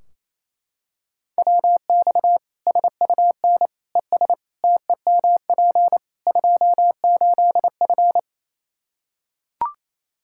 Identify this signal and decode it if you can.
Morse code, heard as a keyed tone, about 28 wpm, 700 Hz. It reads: WX SUN ES TEMP 28F
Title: Practice QSO - Exchange 2 - Info - 28wpm